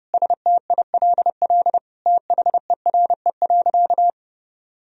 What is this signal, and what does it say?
Morse code, heard as a keyed tone, about 28 wpm, 700 Hz. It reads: STILL THERE.
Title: Sentences from Top 1000 Words - Review Part 1 - 30wpm